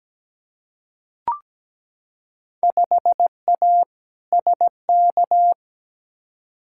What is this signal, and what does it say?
Morse code, heard as a keyed tone, about 17 wpm, 700 Hz. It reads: 5A SK
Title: Field Day Exchanges Encore - 17wpm